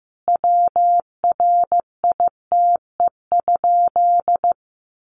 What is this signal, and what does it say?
Morse code, heard as a keyed tone, about 15 wpm, 700 Hz. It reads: WRITE?
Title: Sentences from Top 600 Words 15wpm